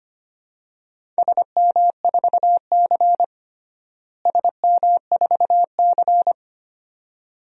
Morse code, heard as a keyed tone, about 25 wpm, 700 Hz. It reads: SM4C SM4C
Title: DXCC-100 Call Signs - 4 Characters - Repeated 2x - 25wpm